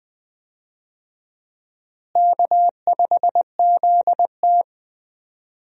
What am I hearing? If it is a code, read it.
Morse code, heard as a keyed tone, about 21 wpm, 700 Hz. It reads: K5ZT
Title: USA Call Signs - 4 Characters - Repeated 2x - 20wpm